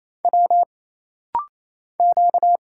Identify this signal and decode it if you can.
Morse code, heard as a keyed tone, about 28 wpm, 700 Hz. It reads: W Q
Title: Single Letter-Number Alt - Rapid-Fire 28wpm